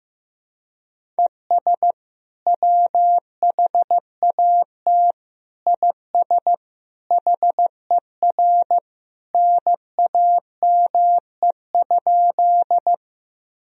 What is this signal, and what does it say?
Morse code, heard as a keyed tone, about 15 wpm, 700 Hz. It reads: ES WHAT IS HER NAME?
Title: Sentences from Top 200 Words and Abbreviations 15wpm